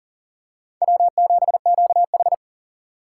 Morse code, heard as a keyed tone, about 40 wpm, 700 Hz. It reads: W7QH
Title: USA Call Signs - 8x Character Spacing - 40wpm